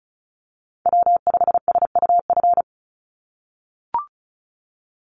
Morse code, heard as a keyed tone, about 36 wpm, 700 Hz. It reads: W5SUF
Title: USA Call Signs 35wpm